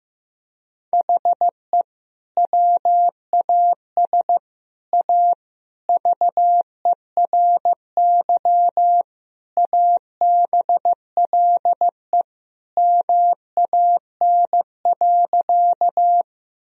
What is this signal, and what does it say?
Morse code, heard as a keyed tone, about 16 wpm, 700 Hz. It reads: HE WAS A VERY ABLE MAN.
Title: Sentences from Top 1000 Words - Review Part 2 - 15wpm